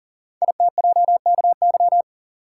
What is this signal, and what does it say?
Morse code, heard as a keyed tone, about 38 wpm, 690 Hz. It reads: ITJKY
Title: Code Groups - 5 Letters - 40wpm